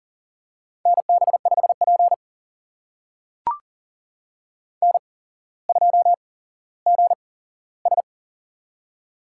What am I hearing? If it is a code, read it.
Morse code, heard as a keyed tone, about 39 wpm, spaced as for 17 wpm, 700 Hz. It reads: NB5P N2GS
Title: USA Call Signs - 4 Characters - 8x Character Spacing - 40wpm